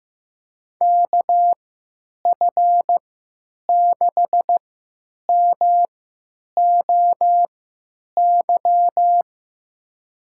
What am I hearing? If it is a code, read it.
Morse code, heard as a keyed tone, about 15 wpm, spaced as for 9 wpm, 700 Hz. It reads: KF6MOY
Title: USA Call Signs - 6 Characters - 3x Character Spacing - 15wpm